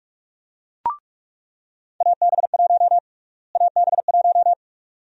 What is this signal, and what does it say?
Morse code, heard as a keyed tone, about 44 wpm, 700 Hz. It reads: AB1 AB1
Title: USA Call Sign Prefixes - Repeated 2x - 45wpm